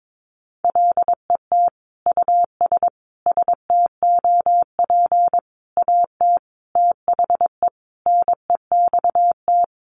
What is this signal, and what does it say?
Morse code, heard as a keyed tone, about 22 wpm, 700 Hz. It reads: LET US STOP AT THE NEXT
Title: Sentences from Top 400 Words 22wpm